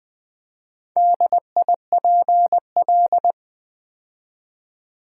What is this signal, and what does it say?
Morse code, heard as a keyed tone, about 20 wpm, 700 Hz. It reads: DIPL
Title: Build A Word - 4 Letters At A Time - 10-Letter Words - 20wpm